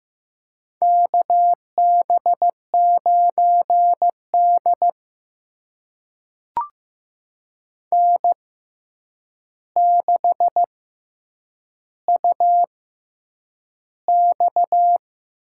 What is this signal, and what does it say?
Morse code, heard as a keyed tone, about 15 wpm, spaced as for 8 wpm, 700 Hz. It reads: KB9D N6UX
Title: USA Call Signs - 4 Characters - 6x Character Spacing - 15wpm